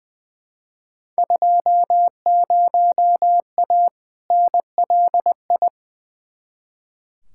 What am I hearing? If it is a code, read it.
Morse code, heard as a keyed tone, about 20 wpm, 700 Hz. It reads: 20A NLI
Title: Field Day Exchanges Encore - 20wpm